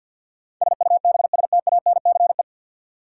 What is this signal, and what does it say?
Morse code, heard as a keyed tone, about 53 wpm, 700 Hz. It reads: SUBSTANCE
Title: Sentences from Top 1500 Words - New Word Drill - 50wpm